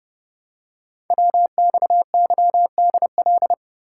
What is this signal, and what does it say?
Morse code, heard as a keyed tone, about 30 wpm, 700 Hz. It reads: WXYDL
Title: Code Groups - 5 Letters - 30wpm